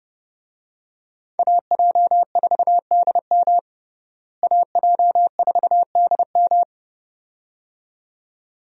Morse code, heard as a keyed tone, about 30 wpm, 700 Hz. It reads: AJ4DM AJ4DM
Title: USA Call Signs - 5 Characters - Repeated 2x - 30wpm